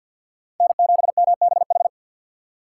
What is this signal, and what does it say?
Morse code, heard as a keyed tone, about 52 wpm, 700 Hz. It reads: N7MBH
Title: USA Call Signs - 5 Characters - 2x Character Spacing - 50wpm